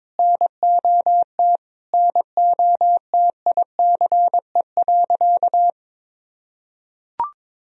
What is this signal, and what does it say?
Morse code, heard as a keyed tone, about 22 wpm, 700 Hz. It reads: NOT NOTICE.
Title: Sentences from Top 1000 Words - Review Part 2 - 22wpm